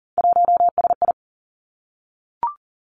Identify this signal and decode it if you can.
Morse code, heard as a keyed tone, about 40 wpm, 700 Hz. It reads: 1SI
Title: Three Challenging Letter-Number Combinations 40wpm